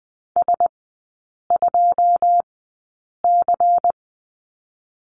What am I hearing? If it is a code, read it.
Morse code, heard as a keyed tone, about 20 wpm, spaced as for 8 wpm, 700 Hz. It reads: S2C
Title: 127 Lesson 21e - Number 6 - Three Characters 20wpm